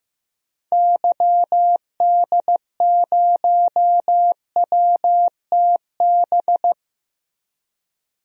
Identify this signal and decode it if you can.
Morse code, heard as a keyed tone, about 15 wpm, 700 Hz. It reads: YD0WTB
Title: DXCC-100 Call Signs - 6 Characters - 15wpm